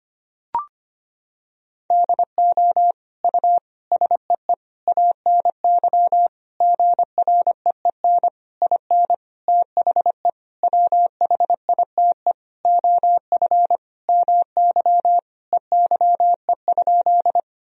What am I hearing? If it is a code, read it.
Morse code, heard as a keyed tone, about 25 wpm, 700 Hz. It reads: DO U SEE ANY GREEN IN THE WHITE OF MY EYE?
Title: Sentences from Top 500 Words and Abbreviations 25wpm